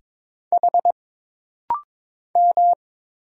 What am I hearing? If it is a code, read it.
Morse code, heard as a keyed tone, about 22 wpm, 700 Hz. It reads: H M
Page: This rapid-fire timing is geared to push your ICR (Instant Character Recognition) skills.